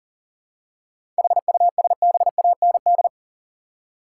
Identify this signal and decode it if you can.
Morse code, heard as a keyed tone, about 40 wpm, 700 Hz. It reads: HUSBAND
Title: Seven Letter Words - 40wpm